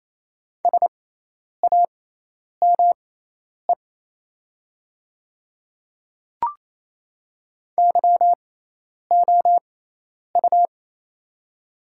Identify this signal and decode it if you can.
Morse code, heard as a keyed tone, about 28 wpm, spaced as for 10 wpm, 700 Hz. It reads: SAME YOU
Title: 6x Character Spacing - Top 200 Words - 28wpm